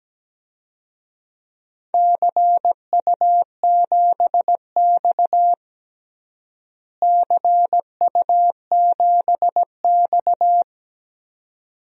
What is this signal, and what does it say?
Morse code, heard as a keyed tone, about 17 wpm, 700 Hz. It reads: CU7X CU7X
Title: DXCC-100 Call Signs - 4 Characters - Repeated 2x - 17wpm